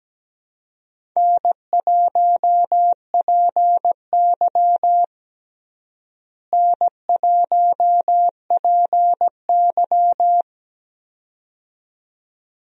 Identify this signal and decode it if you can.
Morse code, heard as a keyed tone, about 17 wpm, 700 Hz. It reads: N1PY N1PY
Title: USA Call Signs - 4 Characters - Repeated 2x - 17wpm